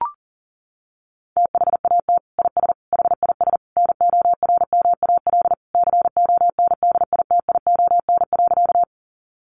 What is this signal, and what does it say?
Morse code, heard as a keyed tone, about 40 wpm, 700 Hz. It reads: THAT IS HIS NORMAL CONDITION.
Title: Sentences from Top 1200 Words 40wpm